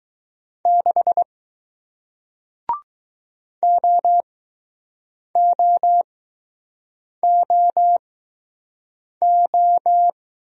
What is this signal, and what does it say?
Morse code, heard as a keyed tone, about 20 wpm, spaced as for 6 wpm, 700 Hz. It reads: 6 OOOO